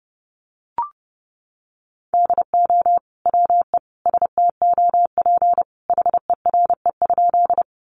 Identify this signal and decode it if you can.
Morse code, heard as a keyed tone, about 30 wpm, 700 Hz. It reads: DO WE STOP HERE?
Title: Sentences from Top 300 Words 30wpm